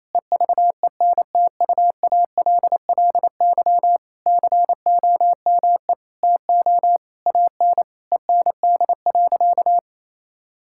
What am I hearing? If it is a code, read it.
Morse code, heard as a keyed tone, about 28 wpm, 700 Hz. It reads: EVENTUALLY COME TO AN END.
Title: Sentences from Top 1000 Words - Review Part 6 - 28wpm